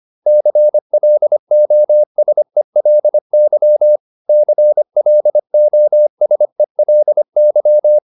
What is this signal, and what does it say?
Morse code, heard as a keyed tone, about 25 wpm, 600 Hz. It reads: CLOSELY CLOSELY